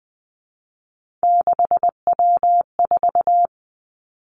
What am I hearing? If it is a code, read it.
Morse code, heard as a keyed tone, about 20 wpm, 700 Hz. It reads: 6W4
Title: Three Challenging Letter-Number Combinations 20wpm